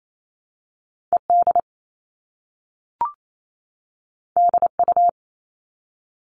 Morse code, heard as a keyed tone, about 28 wpm, 700 Hz. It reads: ED DU
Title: All Cut Numbers - 2-Digit 28wpm